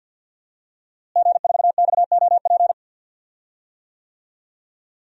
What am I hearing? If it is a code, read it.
Morse code, heard as a keyed tone, about 52 wpm, 700 Hz. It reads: G4XOP
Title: DXCC-100 Call Signs - 5 Characters - Repeated 2x - 50wpm